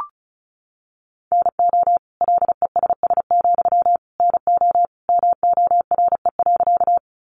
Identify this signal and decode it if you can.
Morse code, heard as a keyed tone, about 34 wpm, 700 Hz. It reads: NO LESS, NO MORE.
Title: Sentences from Top 400 Words 35wpm